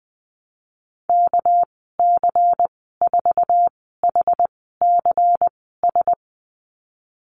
Morse code, heard as a keyed tone, about 19 wpm, spaced as for 14 wpm, 700 Hz. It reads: KC4HCS
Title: 207 Lesson 33g - Letter j - Call Signs 20wpm